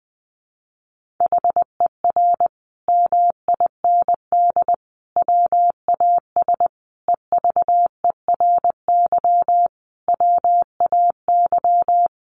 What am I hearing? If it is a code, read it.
Morse code, heard as a keyed tone, about 20 wpm, 700 Hz. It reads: HER MIND WAS EVERY WAY
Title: Sentences from Top 1400 Words 20wpm